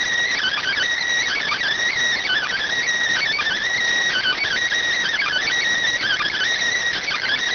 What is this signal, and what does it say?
Emergency Alert System (SAME) tones (header bursts), heard as harsh buzzing bursts. It is heard from 0.3 to 6.4 s.